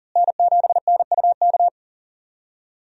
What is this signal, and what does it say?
Morse code, heard as a keyed tone, about 40 wpm, 700 Hz. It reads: N7NUK
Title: USA Call Signs - 5 Characters - Repeated 3x - 40wpm